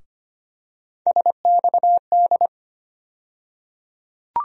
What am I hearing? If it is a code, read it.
Morse code, heard as a keyed tone, about 25 wpm, 700 Hz. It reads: SXD
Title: USA Call Sign Postfixes - 3 Letters 25wpm